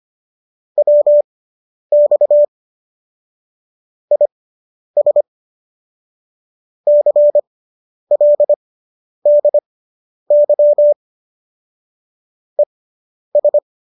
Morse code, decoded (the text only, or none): WX IS CLDY ES